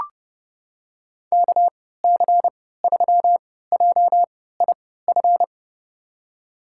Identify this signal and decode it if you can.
Morse code, heard as a keyed tone, about 30 wpm, spaced as for 17 wpm, 700 Hz. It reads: KC3JIF
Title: USA Call Signs - 6 Characters - 3x Character Spacing - 30wpm